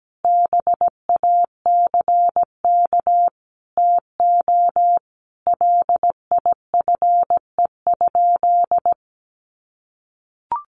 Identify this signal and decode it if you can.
Morse code, heard as a keyed tone, about 17 wpm, 700 Hz. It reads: BACK TO LIFE?
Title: Sentences from Top 1500 Words 17wpm